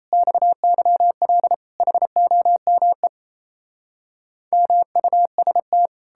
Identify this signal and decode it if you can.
Morse code, heard as a keyed tone, about 37 wpm, 700 Hz. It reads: XYL HOME MUST